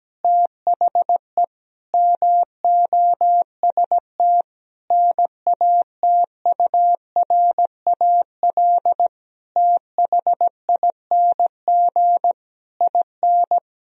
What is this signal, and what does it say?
Morse code, heard as a keyed tone, about 17 wpm, 700 Hz. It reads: THE MOST NATURAL THING IN
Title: Sentences from Top 1000 Words - Review Part 4 - 17wpm